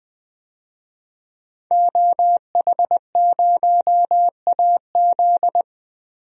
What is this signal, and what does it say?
Morse code, heard as a keyed tone, about 20 wpm, 700 Hz. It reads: OH0AZ
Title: DXCC-100 Call Signs - 5 Characters - Repeated 3x - 20wpm